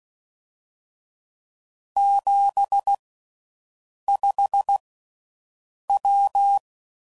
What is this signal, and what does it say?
Morse code, heard as a keyed tone, about 16 wpm, spaced as for 6 wpm, 800 Hz. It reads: 75W